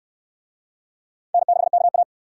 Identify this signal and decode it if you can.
Morse code, heard as a keyed tone, about 66 wpm, 700 Hz. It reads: N6KA